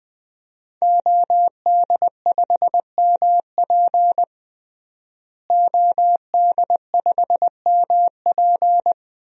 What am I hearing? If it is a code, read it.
Morse code, heard as a keyed tone, about 20 wpm, 700 Hz. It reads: OD5MP OD5MP
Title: DXCC-100 Call Signs - 5 Characters - Repeated 2x - 20wpm